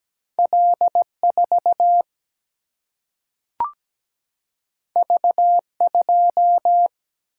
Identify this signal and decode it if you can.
Morse code, heard as a keyed tone, about 17 wpm, 700 Hz. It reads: L 4 V 2
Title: Single-Letter Single-Number 17wpm